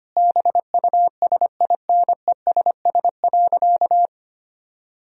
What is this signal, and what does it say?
Morse code, heard as a keyed tone, about 25 wpm, 700 Hz. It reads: BUSINESS.
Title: Sentences from Top 1000 Words - Review Part 4 - 25wpm